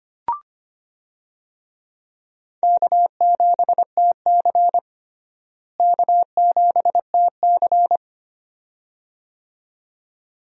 Morse code, heard as a keyed tone, about 25 wpm, 700 Hz. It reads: K7TC K7TC
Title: USA Call Signs - 4 Characters - Repeated 2x - 25wpm